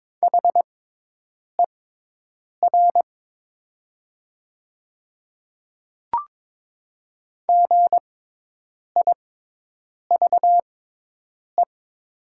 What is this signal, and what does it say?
Morse code, heard as a keyed tone, about 22 wpm, spaced as for 8 wpm, 700 Hz. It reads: HER GIVE
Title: 6x Character Spacing - Top 200 Words - 22wpm